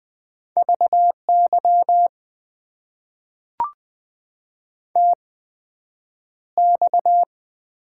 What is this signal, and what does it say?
Morse code, heard as a keyed tone, about 20 wpm, spaced as for 9 wpm, 700 Hz. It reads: VY TX
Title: Top 90 QSO Elements - 8x Character Spacing - 20wpm